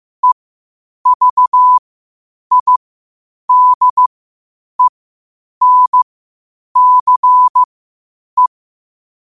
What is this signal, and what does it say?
Morse code, heard as a keyed tone, about 15 wpm, spaced as for 8 wpm, 1000 Hz. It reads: EVIDENCE
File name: qotdfiles / 8wpm